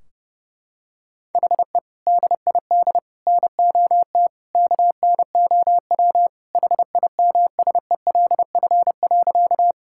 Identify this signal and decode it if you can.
Morse code, heard as a keyed tone, about 30 wpm, 700 Hz. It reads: HE DID NOT KNOW HIMSELF.
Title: Sentences from Top 1600 Words 30wpm